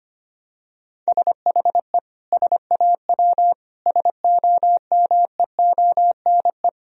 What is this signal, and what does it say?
Morse code, heard as a keyed tone, about 25 wpm, 700 Hz. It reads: SHE SAW SOMEONE
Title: Sentences from Top 1000 Words - Review Part 4 - 25wpm